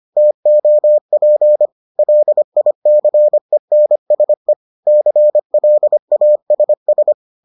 Morse code, heard as a keyed tone, about 25 wpm, 600 Hz. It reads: TOP LICENSE CLASS